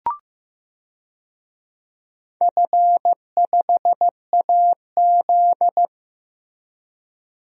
Morse code, heard as a keyed tone, about 16 wpm, 700 Hz. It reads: F5AZ
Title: DXCC-100 Call Signs - 4 Characters - Repeated 2x - 15wpm